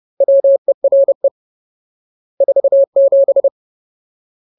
Morse code, heard as a keyed tone, about 30 wpm, 550 Hz. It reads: WERE 47
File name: Ultimate Word List / 30WPM Ultimate